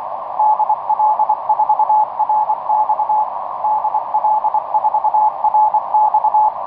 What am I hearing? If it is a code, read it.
Morse code, heard as a keyed tone, about 24 wpm, 860 Hz, with radio noise. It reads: DL4RX DL4RX
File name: dp1pol dl3kum etc 15m cw 17oct2009 1546